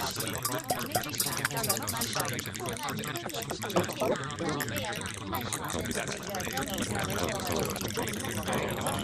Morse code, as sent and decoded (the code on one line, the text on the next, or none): none